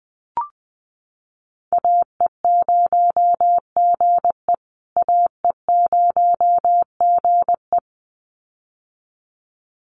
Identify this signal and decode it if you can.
Morse code, heard as a keyed tone, about 20 wpm, 700 Hz. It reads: AE0GE AE0GE
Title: USA Call Signs Repeated 2x 20wpm